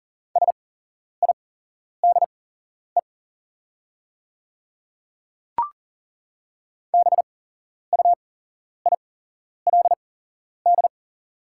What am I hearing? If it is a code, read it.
Morse code, heard as a keyed tone, about 40 wpm, spaced as for 11 wpm, 690 Hz. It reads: SIDE BUILD
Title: 8x Character Spacing - Top 200 Words - 40wpm